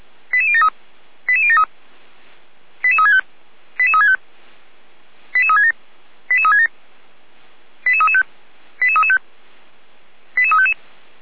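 Five-tone selective call (ZVEI), heard as quick runs of five tones, tones of about 80 ms. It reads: 88073 88073 88365 88365 88367 88367 88335 88335 88355
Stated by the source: ZVEI five-tone selective call received by SDR.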